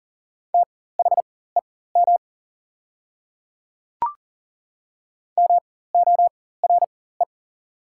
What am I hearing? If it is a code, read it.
Morse code, heard as a keyed tone, about 39 wpm, spaced as for 19 wpm, 700 Hz. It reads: THEM MORE
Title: 4x Character Spacing - Top 100 Words - 40wpm